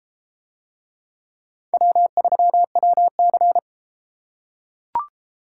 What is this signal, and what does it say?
Morse code, heard as a keyed tone, about 33 wpm, 700 Hz. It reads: W3WC